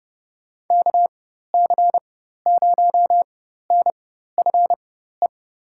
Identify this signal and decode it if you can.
Morse code, heard as a keyed tone, about 30 wpm, spaced as for 14 wpm, 700 Hz. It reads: KC0NFE
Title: USA Call Signs - 6 Characters - 4x Character Spacing - 30wpm